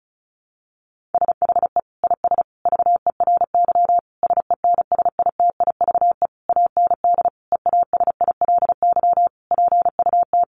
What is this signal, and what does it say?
Morse code, heard as a keyed tone, about 34 wpm, 700 Hz. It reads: SHE IS VERY SENSITIVE AND EASILY PUT
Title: Sentences from Top 1400 Words 35wpm